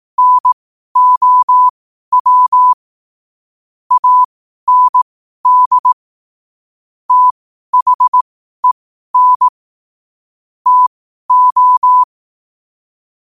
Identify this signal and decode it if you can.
Morse code, heard as a keyed tone, about 18 wpm, spaced as for 12 wpm, 1000 Hz. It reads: NOW AND THEN TO
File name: qotdfiles / 12wpm